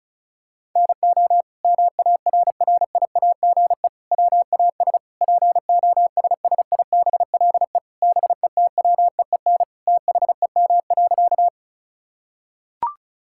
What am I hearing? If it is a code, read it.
Morse code, heard as a keyed tone, about 36 wpm, 700 Hz. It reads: NO MARRIAGE WAS POSSIBLE BETWEEN THEM.
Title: Sentences from Top 1000 Words - Review Part 5 - 35wpm